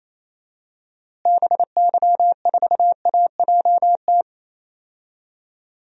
Morse code, heard as a keyed tone, about 27 wpm, 700 Hz. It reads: BY4AJT
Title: DXCC-100 Call Signs - 6 Characters - Repeated 2x - 28wpm